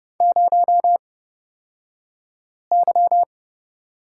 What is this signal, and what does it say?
Morse code, heard as a keyed tone, about 29 wpm, spaced as for 5 wpm, 690 Hz. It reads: 0Y
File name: Lesson 36d - BT - Two Characters 30wpm